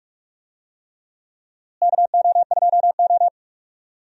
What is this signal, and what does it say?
Morse code, heard as a keyed tone, about 42 wpm, 700 Hz. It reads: KO2O
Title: USA Call Signs - 4 Characters - Repeated 3x - 45wpm